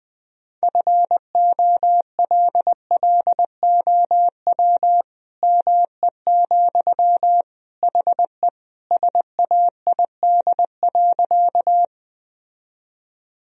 Morse code, heard as a keyed tone, about 20 wpm, 700 Hz. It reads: FOLLOW ME, HE SAID.
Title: Sentences from Top 1000 Words - Review Part 1 - 20wpm